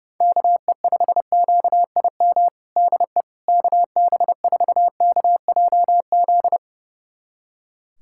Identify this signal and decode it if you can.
Morse code, heard as a keyed tone, about 30 wpm, 700 Hz. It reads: KE5QIM DE KB4KJZ
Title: USA Callsign DE Callsign - 6 Characters - 30wpm